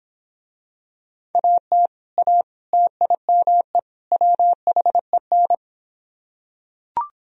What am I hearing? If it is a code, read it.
Morse code, heard as a keyed tone, about 25 wpm, 700 Hz. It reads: AT A TIME WHEN